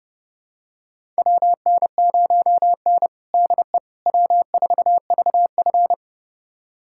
Morse code, heard as a keyed tone, about 30 wpm, 700 Hz. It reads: WN0N DE W4VF
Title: USA Callsign DE Callsign - 4 Characters - 30wpm